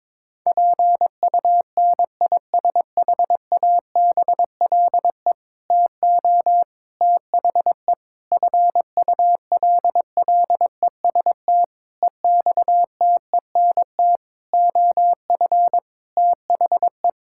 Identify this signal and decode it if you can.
Morse code, heard as a keyed tone, about 21 wpm, 700 Hz. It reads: PUNISHABLE TO THE FULLEST EXTENT OF THE
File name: Sentences with 7-Word Phrases - 22wpm